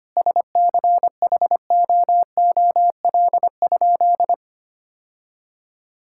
Morse code, heard as a keyed tone, about 25 wpm, 700 Hz. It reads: SCHOOL?
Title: Sentences from Top 1000 Words - Review Part 5 - 25wpm